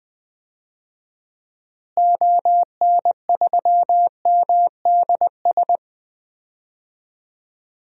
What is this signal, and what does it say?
Morse code, heard as a keyed tone, about 20 wpm, 700 Hz. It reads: ON3MDS
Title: DXCC-100 Call Signs - 6 Characters - Repeated 2x - 20wpm